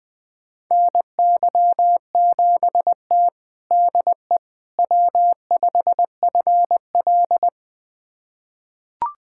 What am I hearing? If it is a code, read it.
Morse code, heard as a keyed tone, about 20 wpm, 700 Hz. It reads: NY7T DE W5FL
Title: USA Callsign DE Callsign - 4 Characters - 20wpm